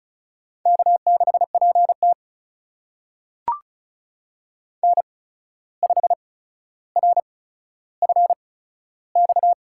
Morse code, heard as a keyed tone, about 34 wpm, spaced as for 14 wpm, 700 Hz. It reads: K6PT N5RFX
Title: USA Call Signs - 8x Character Spacing - 35wpm